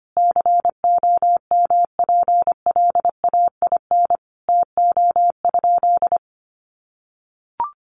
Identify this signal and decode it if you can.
Morse code, heard as a keyed tone, about 25 wpm, 700 Hz. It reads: COMPLAIN TO?
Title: Sentences from Top 1300 Words 25wpm